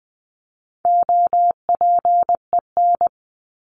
Morse code, heard as a keyed tone, about 20 wpm, 700 Hz. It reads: OPEN
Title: Top 300 Words 20wpm